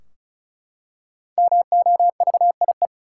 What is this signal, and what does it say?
Morse code, heard as a keyed tone, about 36 wpm, 700 Hz. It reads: MOVIE
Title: Five Letter Words - 35wpm